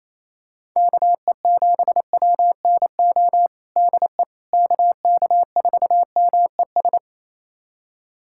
Morse code, heard as a keyed tone, about 28 wpm, 700 Hz. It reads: KE7WNO DE KK4MES
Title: USA Callsign DE Callsign - 6 Characters - 28wpm